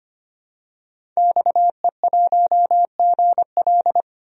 Morse code, heard as a keyed tone, about 25 wpm, 700 Hz. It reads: XE1GL
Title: DXCC-100 Call Signs - 5 Characters - 25wpm